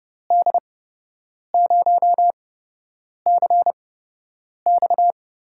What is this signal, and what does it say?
Morse code, heard as a keyed tone, about 30 wpm, spaced as for 8 wpm, 700 Hz. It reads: D0CX
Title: USA Call Signs - 6 Characters - 8x Character Spacing - 30wpm